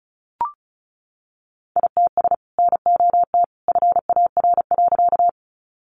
Morse code, heard as a keyed tone, about 34 wpm, 700 Hz. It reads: ITS NOT FAR.
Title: Sentences from Top 600 Words 35wpm